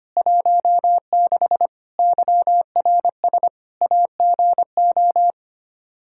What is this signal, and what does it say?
Morse code, heard as a keyed tone, about 25 wpm, 700 Hz. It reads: 16 YRS AGO
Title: Practice QSO - Exchange 3 - More Info - 25wpm